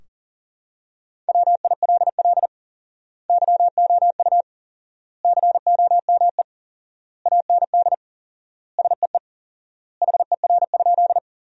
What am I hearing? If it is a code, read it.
Morse code, heard as a keyed tone, about 40 wpm, 700 Hz. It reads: WILL YOU COME AND SEE HER?
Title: 4x Word-Spacing - Sentences from Top 100 Words - 40wpm